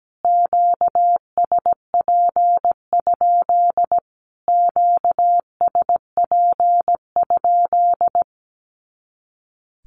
Morse code, heard as a keyed tone, about 17 wpm, 700 Hz. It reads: QSP? QSP?
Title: Q-Codes Repeated 3x 17wpm